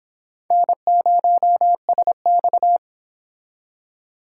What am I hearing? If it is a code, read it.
Morse code, heard as a keyed tone, about 26 wpm, 700 Hz. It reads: N0SX